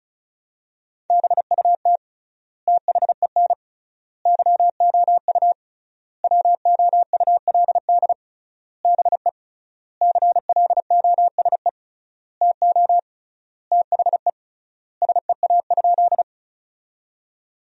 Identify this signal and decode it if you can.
Morse code, heard as a keyed tone, about 34 wpm, 700 Hz. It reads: BUT THEN YOU WOULD BE CLOSE TO THE SEA?
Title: 3x Word-Spacing - Sentences from Top 300 Words - 35wpm